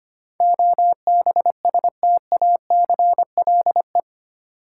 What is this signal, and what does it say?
Morse code, heard as a keyed tone, about 25 wpm, 700 Hz. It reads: OBSTACLE
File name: 8 Letter Words - First 3 Letters Given - 25wpm